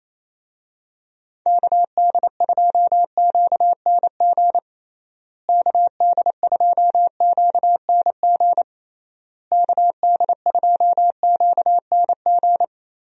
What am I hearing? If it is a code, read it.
Morse code, heard as a keyed tone, about 28 wpm, 700 Hz. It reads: KD2QNG KD2QNG KD2QNG
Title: USA Call Signs - 6 Characters - Repeated 3x - 28wpm